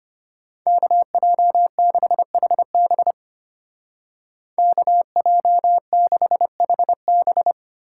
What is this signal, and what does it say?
Morse code, heard as a keyed tone, about 31 wpm, 700 Hz. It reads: KJ6HB KJ6HB
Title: Speed Racing - USA Call Signs 25wpm